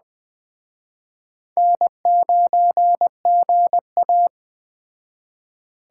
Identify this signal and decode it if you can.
Morse code, heard as a keyed tone, about 21 wpm, 700 Hz. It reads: N9GA